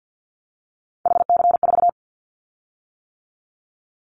Morse code, heard as a keyed tone, about 50 wpm, 700 Hz. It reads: HC4
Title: Callsign Prefixes for Top 100 DXCC - 50wpm